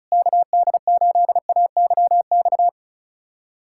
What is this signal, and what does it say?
Morse code, heard as a keyed tone, about 36 wpm, 700 Hz. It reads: KD8AYX
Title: USA Callsign DE Callsign 35wpm